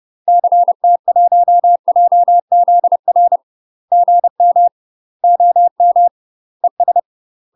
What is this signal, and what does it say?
Morse code, heard as a keyed tone, about 30 wpm, 700 Hz. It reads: CT1JZR GM OM ES
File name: UK MASTER QSOs TRAINING / 30WPM / UK MASTER QSOs 0026